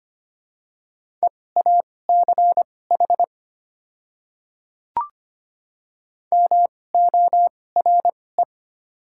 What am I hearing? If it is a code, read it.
Morse code, heard as a keyed tone, about 25 wpm, spaced as for 18 wpm, 700 Hz. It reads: EACH MORE
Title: 2x Character Spacing - Top 100 Words - 25wpm